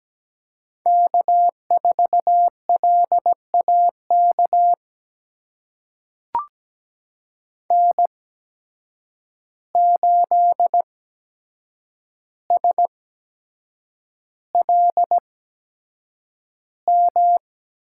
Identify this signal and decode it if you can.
Morse code, heard as a keyed tone, about 17 wpm, spaced as for 7 wpm, 700 Hz. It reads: K4LAK N8SLM
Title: USA Call Signs - 5 Characters - 8x Character Spacing - 17wpm